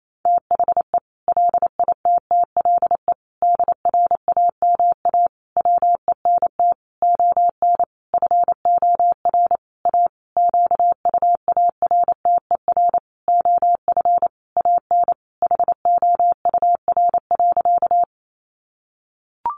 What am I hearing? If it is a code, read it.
Morse code, heard as a keyed tone, about 28 wpm, 700 Hz. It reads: THE LITTLE DRAMA WENT ON FOR A QUARTER OF AN HOUR.
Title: Sentences from Top 1500 Words 28wpm